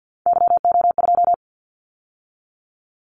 Morse code, heard as a keyed tone, about 50 wpm, 700 Hz. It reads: YO2
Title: Callsign Prefixes for Top 25 DXCC - 50wpm